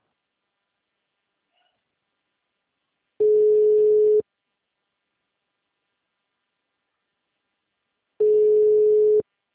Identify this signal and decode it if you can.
Ringback tone (Europe, 425 Hz).